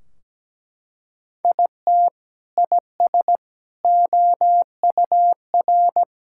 Morse code, heard as a keyed tone, about 17 wpm, 700 Hz. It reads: IT IS OUR
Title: Sentences from Top 1600 Words 17wpm